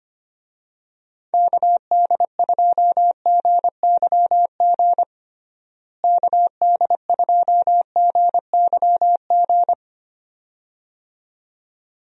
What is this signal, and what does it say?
Morse code, heard as a keyed tone, about 25 wpm, 700 Hz. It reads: KD2GYG KD2GYG
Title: USA Call Signs - 6 Characters - Repeated 2x - 25wpm